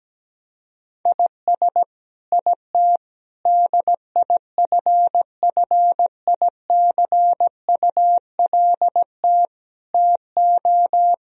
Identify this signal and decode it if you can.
Morse code, heard as a keyed tone, about 17 wpm, 700 Hz. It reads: IS IT DIFFICULT TO
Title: Sentences from Top 1000 Words - Review Part 3 - 17wpm